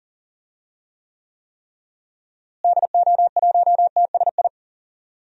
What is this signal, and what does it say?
Morse code, heard as a keyed tone, about 40 wpm, 700 Hz. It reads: DO1TSI
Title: DXCC-100 Call Signs - 6 Characters - Repeated 3x - 40wpm